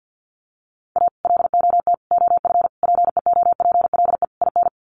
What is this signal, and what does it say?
Morse code, heard as a keyed tone, about 48 wpm, 700 Hz. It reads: A LOT OF PEOPLE IN
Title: Sets of 5 Words 50wpm